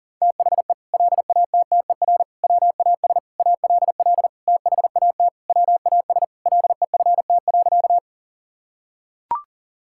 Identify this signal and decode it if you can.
Morse code, heard as a keyed tone, about 40 wpm, 700 Hz. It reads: THE LATTER WAS ALL THAT WAS LEFT.
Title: Sentences from Top 1600 Words 40wpm